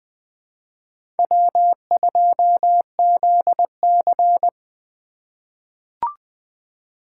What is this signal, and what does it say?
Morse code, heard as a keyed tone, about 20 wpm, 700 Hz. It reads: W2ZC
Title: USA Call Signs - 4 Characters - 2x Character Spacing - 20wpm